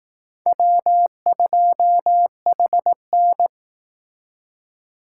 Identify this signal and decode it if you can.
Morse code, heard as a keyed tone, about 18 wpm, 700 Hz. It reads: W2HN
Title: Speed Racing - USA Call Signs 15wpm